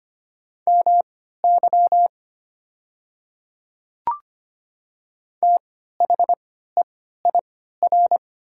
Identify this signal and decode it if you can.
Morse code, heard as a keyed tone, about 25 wpm, spaced as for 14 wpm, 700 Hz. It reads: MY THEIR
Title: 3x Character Spacing - Top 100 Words - 25wpm